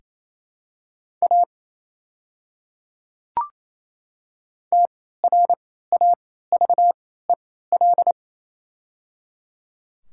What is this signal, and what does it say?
Morse code, heard as a keyed tone, about 28 wpm, spaced as for 16 wpm, 700 Hz. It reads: A TRAVEL
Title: 3x Character Spacing - Top 500 Words - 28wpm